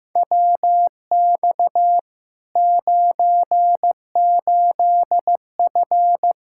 Morse code, heard as a keyed tone, about 15 wpm, 700 Hz. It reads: WX 98F
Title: Practice QSO - Exchange 2 - Info - 15wpm